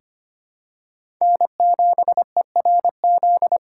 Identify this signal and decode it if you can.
Morse code, heard as a keyed tone, about 25 wpm, 700 Hz. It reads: N7ERZ
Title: USA Call Signs - 5 Characters - 2x Character Spacing - 25wpm